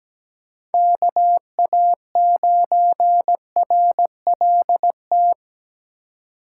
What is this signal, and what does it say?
Morse code, heard as a keyed tone, about 17 wpm, 700 Hz. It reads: KA9RLT
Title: USA Call Signs - 6 Characters - 4x Character Spacing - 17wpm